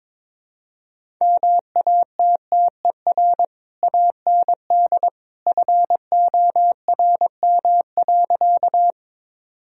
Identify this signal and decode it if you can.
Morse code, heard as a keyed tone, about 21 wpm, 700 Hz. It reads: MATTER AND FORM.
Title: Sentences from Top 1000 Words - Review Part 1 - 22wpm